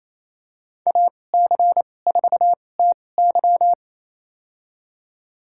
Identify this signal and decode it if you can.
Morse code, heard as a keyed tone, about 28 wpm, spaced as for 20 wpm, 700 Hz. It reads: AC4TY
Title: USA Call Signs - 5 Characters - 2x Character Spacing - 28wpm